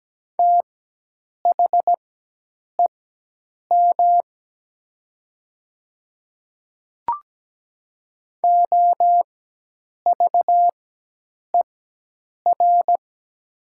Morse code, heard as a keyed tone, about 17 wpm, spaced as for 8 wpm, 700 Hz. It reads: THEM OVER
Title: 4x Character Spacing - Top 100 Words - 17wpm